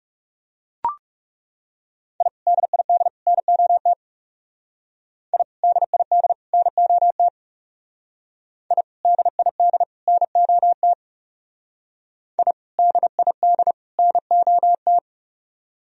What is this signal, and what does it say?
Morse code, heard as a keyed tone, about 38 wpm, 700 Hz. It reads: I DID NOT I DID NOT I DID NOT I DID NOT